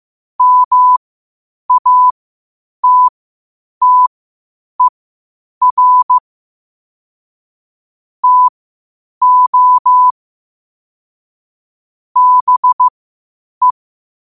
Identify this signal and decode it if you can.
Morse code, heard as a keyed tone, about 15 wpm, spaced as for 8 wpm, 1000 Hz. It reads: MATTER TO BE
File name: qotdfiles / 8wpm